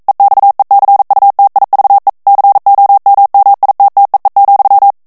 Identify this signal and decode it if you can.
Morse code, heard as a keyed tone, about 42 wpm, 800 Hz. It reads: EXECUTIVE COMMITTEE,
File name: 40 mots minute